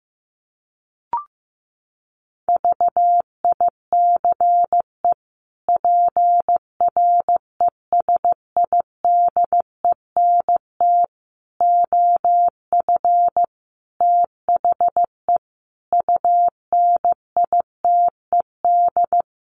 Morse code, heard as a keyed tone, about 14 wpm, 700 Hz. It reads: VICE PRESIDENT OF THE UNITED
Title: Sets of 5 Words Encore2 15wpm